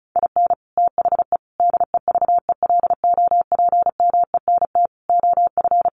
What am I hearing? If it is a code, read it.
Morse code, heard as a keyed tone, about 34 wpm, 700 Hz. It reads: IN THE DEVELOPMENT OF
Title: Sets of 5 Words Encore2 35wpm